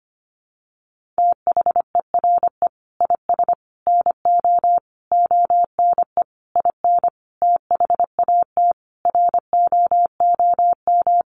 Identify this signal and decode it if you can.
Morse code, heard as a keyed tone, about 25 wpm, 700 Hz. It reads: THERE IS NO ONE IN THAT ROOM
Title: Sentences from Top 500 Words 25wpm